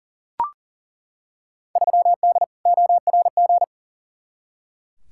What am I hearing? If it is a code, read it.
Morse code, heard as a keyed tone, about 40 wpm, 700 Hz. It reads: 3D ORG
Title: Field Day Exchanges Encore - 40wpm